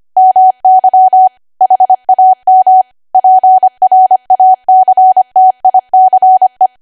Morse code, heard as a keyed tone, about 25 wpm, 750 Hz. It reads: MY HAM PRACTICE
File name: Archive / 25 WPM / 25WPM